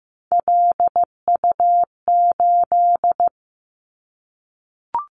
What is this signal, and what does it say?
Morse code, heard as a keyed tone, about 15 wpm, 700 Hz. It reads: LU8
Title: Callsign Prefixes for Top 100 DXCC - 15wpm